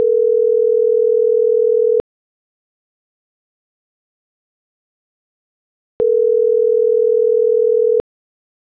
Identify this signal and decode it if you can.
Ringback tone (North America), 440 + 480 Hz.